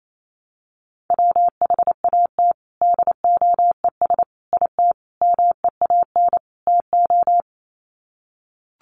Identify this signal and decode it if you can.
Morse code, heard as a keyed tone, about 28 wpm, 700 Hz. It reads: WHAT DOES IT MEAN TO
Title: Sets of 5 Words Encore2 28wpm